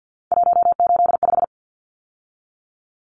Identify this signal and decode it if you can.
Morse code, heard as a keyed tone, about 50 wpm, 700 Hz. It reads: 185
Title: Numbers - 3-Digit 50wpm